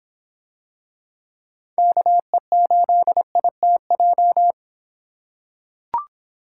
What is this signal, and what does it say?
Morse code, heard as a keyed tone, about 25 wpm, 700 Hz. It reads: KE8ITJ